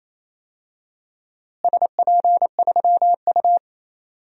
Morse code, heard as a keyed tone, about 28 wpm, 700 Hz. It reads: SP3U
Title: DXCC-100 Call Signs - 4 Characters - Repeated 3x - 28wpm